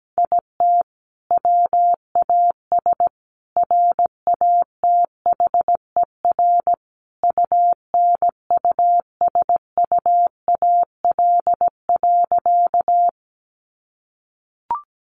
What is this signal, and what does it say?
Morse code, heard as a keyed tone, about 17 wpm, 700 Hz. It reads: IT WAS RATHER UNUSUAL.
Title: Sentences from Top 1400 Words 17wpm